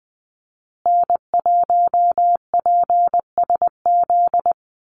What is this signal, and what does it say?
Morse code, heard as a keyed tone, about 20 wpm, 700 Hz. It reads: N1PSZ
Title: USA Call Signs Repeated 3x 20wpm